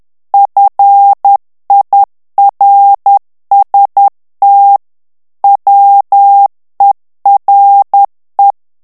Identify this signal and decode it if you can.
Morse code, heard as a keyed tone, about 10 wpm, 800 Hz. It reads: FIRST WERE
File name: ANGLAIS texte clair 3 vitesse 10 mots minute